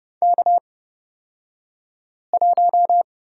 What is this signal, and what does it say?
Morse code, heard as a keyed tone, about 31 wpm, spaced as for 5 wpm, 700 Hz. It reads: K1
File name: Lesson 37d - Letter x - Two Characters 30wpm